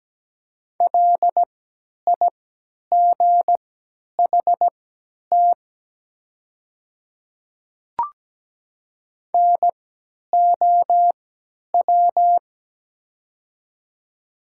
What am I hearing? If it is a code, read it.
Morse code, heard as a keyed tone, about 17 wpm, spaced as for 10 wpm, 700 Hz. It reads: LIGHT NOW
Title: 3x Character Spacing - Top 200 Words - 17wpm